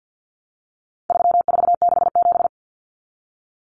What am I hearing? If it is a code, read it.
Morse code, heard as a keyed tone, about 50 wpm, 700 Hz. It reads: 3467
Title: Numbers - 4-Digit 50wpm